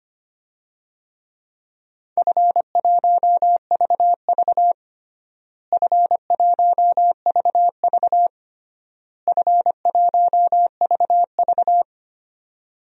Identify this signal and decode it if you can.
Morse code, heard as a keyed tone, about 25 wpm, 700 Hz. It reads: F1VV F1VV F1VV
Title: DXCC-100 Call Signs - 4 Characters - Repeated 3x - 25wpm